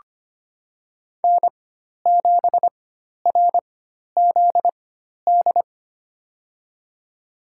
Morse code, heard as a keyed tone, about 25 wpm, spaced as for 12 wpm, 700 Hz. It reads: N7RZD
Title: USA Call Signs - 4x Character Spacing - 25wpm